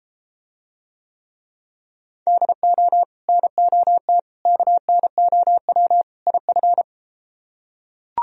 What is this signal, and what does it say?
Morse code, heard as a keyed tone, about 32 wpm, 700 Hz. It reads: DO NOT KNOW IF